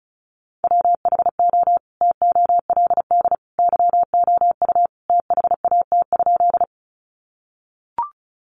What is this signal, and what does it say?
Morse code, heard as a keyed tone, about 34 wpm, 700 Hz. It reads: WHO TOLD YOU THAT?
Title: Sentences from Top 400 Words 35wpm